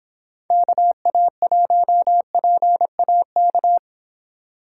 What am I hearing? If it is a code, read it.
Morse code, heard as a keyed tone, about 25 wpm, 700 Hz. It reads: KA1PAK